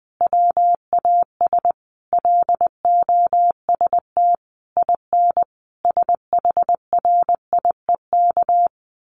Morse code, heard as a keyed tone, about 20 wpm, 700 Hz. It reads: WAS LOST IN SHRIEK
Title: A Sound Of Thunder 20wpm